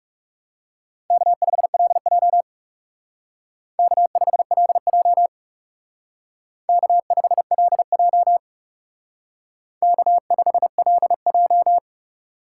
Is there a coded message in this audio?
Yes: Morse code from 1.1 to 11.8 s, heard as a keyed tone, about 38 wpm, 700 Hz. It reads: K5LJ K5LJ K5LJ K5LJ